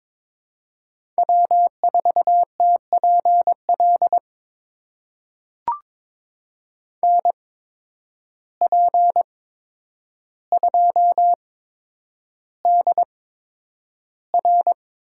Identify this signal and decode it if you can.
Morse code, heard as a keyed tone, about 22 wpm, spaced as for 9 wpm, 700 Hz. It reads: W4TPL NP2DR
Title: USA Call Signs - 5 Characters - 8x Character Spacing - 22wpm